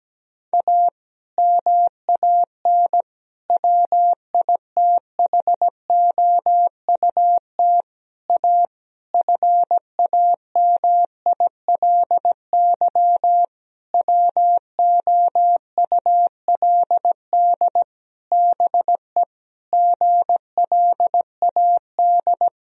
Morse code, heard as a keyed tone, about 17 wpm, 700 Hz. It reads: A MAN WITHOUT A FAMILY WOULD BE GLAD
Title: Sentences from Top 1000 Words - Review Part 3 - 17wpm